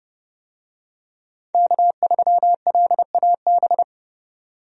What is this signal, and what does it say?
Morse code, heard as a keyed tone, about 30 wpm, 700 Hz. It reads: K3LAB
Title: USA Call Signs - 5 Characters - 30wpm